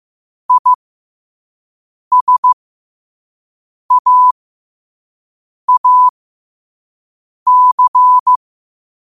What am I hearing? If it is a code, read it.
Morse code, heard as a keyed tone, about 15 wpm, spaced as for 5 wpm, 1000 Hz. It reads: ISAAC
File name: qotdfiles / 5wpm